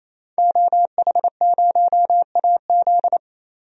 Morse code, heard as a keyed tone, about 28 wpm, 700 Hz. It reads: OH0AZ
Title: DXCC-100 Call Signs - 5 Characters - Repeated 3x - 28wpm